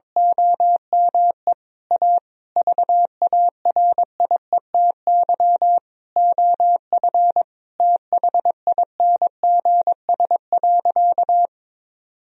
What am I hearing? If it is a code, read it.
Morse code, heard as a keyed tone, about 22 wpm, 700 Hz. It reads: OME A VARIETY OF THINGS.
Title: Sentences from Top 1000 Words - Review Part 4 - 22wpm